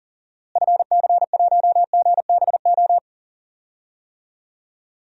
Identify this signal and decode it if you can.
Morse code, heard as a keyed tone, about 40 wpm, 700 Hz. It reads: FC1GBO
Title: DXCC-100 Call Signs - 6 Characters - Repeated 2x - 40wpm